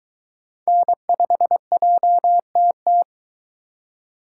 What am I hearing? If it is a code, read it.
Morse code, heard as a keyed tone, about 22 wpm, 700 Hz. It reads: N5JTT